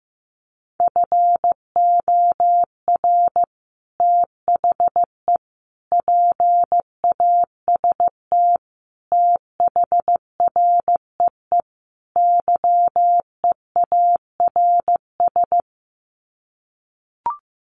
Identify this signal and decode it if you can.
Morse code, heard as a keyed tone, about 15 wpm, 700 Hz. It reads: FOR THE PAST THREE YEARS
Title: Sets of 5 Words Encore2 15wpm